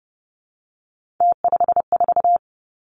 Morse code, heard as a keyed tone, about 30 wpm, 700 Hz. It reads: T54
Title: Three Challenging Letter-Number Combinations 30wpm